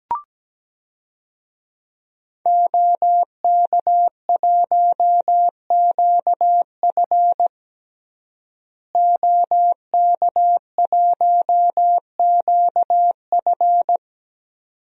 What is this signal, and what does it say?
Morse code, heard as a keyed tone, about 17 wpm, 700 Hz. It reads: OK1QF OK1QF
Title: DXCC-100 Call Signs - 5 Characters - Repeated 2x - 17wpm